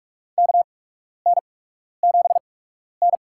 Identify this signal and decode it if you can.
Morse code, heard as a keyed tone, about 47 wpm, spaced as for 12 wpm, 690 Hz. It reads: KN7N
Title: USA Call Signs - 4 Characters - 8x Character Spacing - 45wpm